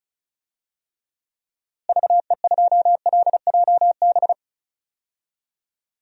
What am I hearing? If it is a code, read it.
Morse code, heard as a keyed tone, about 36 wpm, 700 Hz. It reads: VE2LJB
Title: DXCC-100 Call Signs - 6 Characters - Repeated 2x - 35wpm